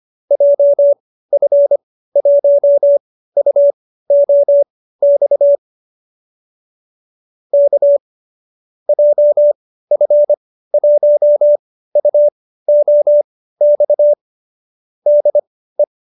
Morse code, heard as a keyed tone, about 25 wpm, spaced as for 15 wpm, 590 Hz. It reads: JF1UOX K JF1UOX DE